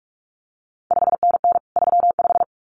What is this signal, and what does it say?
Morse code, heard as a keyed tone, about 46 wpm, 700 Hz. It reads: 5NN 35
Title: CQ WW DX Contest Exchange Encore - 45wpm